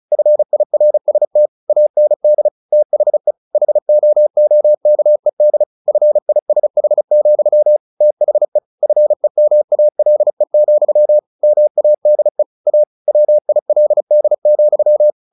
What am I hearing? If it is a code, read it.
Morse code, heard as a keyed tone, about 34 wpm, 600 Hz. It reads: FIRST AND THE HOOKED FISH, THE FEMALE, MADE A WILD,
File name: the old man and the sea / the old man and the sea 1 35wpm